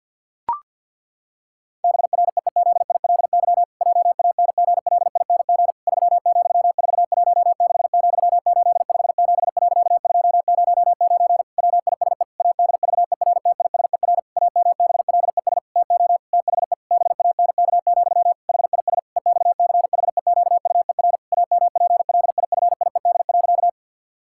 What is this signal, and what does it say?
Morse code, heard as a keyed tone, about 50 wpm, 700 Hz. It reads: BREEZILY JANGLING 3,416,857.209 WISE ADVERTISER AMBLES TO THE BANK, HIS EXCHEQUER AMPLIFIED.
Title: Pangram Challenge v2 - 50wpm